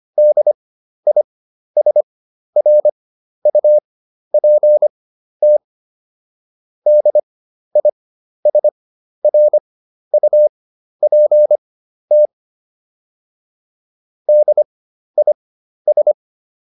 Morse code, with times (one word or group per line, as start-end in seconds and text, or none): 0.2-5.6 DISRUPT
6.9-12.3 DISRUPT
14.3-16.1 DIS